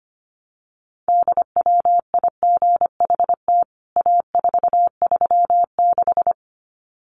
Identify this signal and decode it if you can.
Morse code, heard as a keyed tone, about 25 wpm, 700 Hz. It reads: DWIGHT A436
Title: CWT Exchange 25wpm